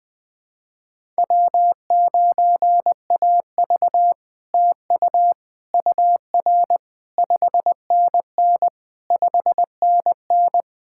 Morse code, heard as a keyed tone, about 20 wpm, 700 Hz. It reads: W9AV TU UR 5NN 5NN
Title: POTA QSO Practice - 20wpm